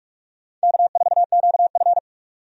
Morse code, heard as a keyed tone, about 41 wpm, 700 Hz. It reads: K4QF
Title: USA Call Signs - 4 Characters - 45wpm